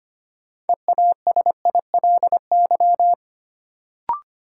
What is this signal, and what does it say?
Morse code, heard as a keyed tone, about 25 wpm, 700 Hz. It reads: EASILY
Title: Sentences from Top 900 Words - New Word Drill - 25wpm